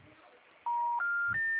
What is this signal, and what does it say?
Special information tone, with speech.